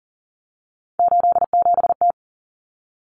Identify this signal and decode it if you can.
Morse code, heard as a keyed tone, about 40 wpm, 700 Hz. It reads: 87T
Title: Common Cut Numbers - 3-Digit 40wpm